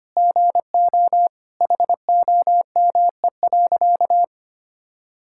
Morse code, heard as a keyed tone, about 25 wpm, 700 Hz. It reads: GO HOME.
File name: practice / 25wpm / Sentences with 3-Word Phrases - 25wpm